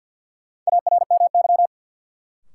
Morse code, heard as a keyed tone, about 48 wpm, 700 Hz. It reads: ARMY
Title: Sentences from Top 800 Words - New Word Drill - 50wpm